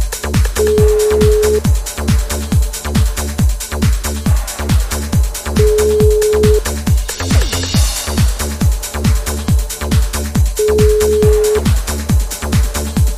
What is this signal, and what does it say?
Ringback tone (Europe, 425 Hz), with noise.